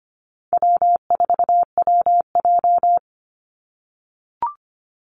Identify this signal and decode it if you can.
Morse code, heard as a keyed tone, about 25 wpm, 700 Hz. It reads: W4WJ
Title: Easy Call Signs - 25wpm